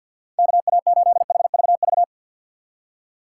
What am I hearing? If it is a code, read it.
Morse code, heard as a keyed tone, about 52 wpm, 700 Hz. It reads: KA8HVV
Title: USA Call Signs - 6 Characters - 4x Character Spacing - 50wpm